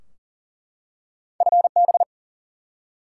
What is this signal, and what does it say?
Morse code, heard as a keyed tone, about 39 wpm, 700 Hz. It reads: FB
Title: Top 180 QSO Elements 40wpm